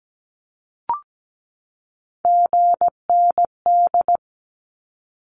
Morse code, heard as a keyed tone, about 17 wpm, 700 Hz. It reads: GND
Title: Abbreviations 17wpm